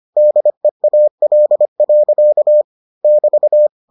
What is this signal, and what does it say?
Morse code, heard as a keyed tone, about 25 wpm, 600 Hz. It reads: DEAL. <BT>